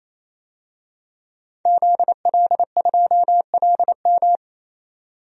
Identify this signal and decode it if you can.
Morse code, heard as a keyed tone, about 28 wpm, 700 Hz. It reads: ZL2LM
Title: DXCC-100 Call Signs - 5 Characters - 28wpm